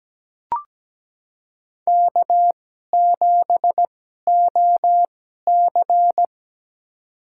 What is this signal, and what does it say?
Morse code, heard as a keyed tone, about 17 wpm, spaced as for 12 wpm, 700 Hz. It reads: K7OC
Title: USA Call Signs - 4 Characters - 2x Character Spacing - 17wpm